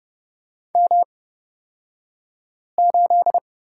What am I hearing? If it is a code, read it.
Morse code, heard as a keyed tone, about 30 wpm, spaced as for 5 wpm, 700 Hz. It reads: M8
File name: Lesson 37d - Letter x - Two Characters 30wpm